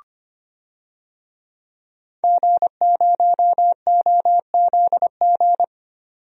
Morse code, heard as a keyed tone, about 25 wpm, 700 Hz. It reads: G0OZG
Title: DXCC-100 Call Signs - 5 Characters - 25wpm